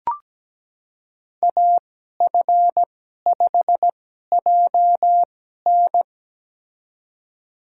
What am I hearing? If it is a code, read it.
Morse code, heard as a keyed tone, about 17 wpm, spaced as for 12 wpm, 700 Hz. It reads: AF5JN
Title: USA Call Signs - 5 Characters - 2x Character Spacing - 17wpm